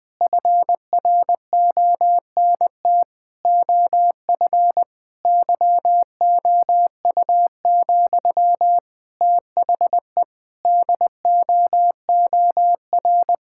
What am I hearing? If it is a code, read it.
Morse code, heard as a keyed tone, about 21 wpm, 700 Hz. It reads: FRONT OF YOU, THE DOOR
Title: Sentences from Top 1000 Words - Review Part 6 - 20wpm